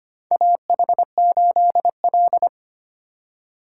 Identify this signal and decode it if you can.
Morse code, heard as a keyed tone, about 25 wpm, 700 Hz. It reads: AH8L
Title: USA Call Signs - 4 Characters - 3x Character Spacing - 25wpm